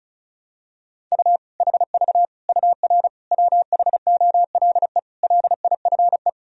Morse code, heard as a keyed tone, about 34 wpm, 700 Hz. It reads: U HV UR WHOLE LIFE
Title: Sentences from Top 400 Words and Abbreviations 35wpm